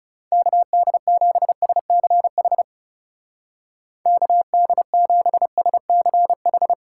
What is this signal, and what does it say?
Morse code, heard as a keyed tone, about 38 wpm, 700 Hz. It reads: KD7SCH KD7SCH